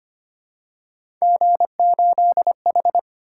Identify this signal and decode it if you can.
Morse code, heard as a keyed tone, about 29 wpm, 700 Hz. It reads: G8H
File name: Speed-Racing - Three Letter-Number Combinations - 25wpm